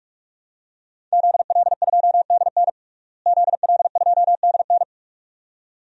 Morse code, heard as a keyed tone, about 46 wpm, 700 Hz. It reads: ZL2DN ZL2DN
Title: DXCC-100 Call Signs - 5 Characters - Repeated 2x - 45wpm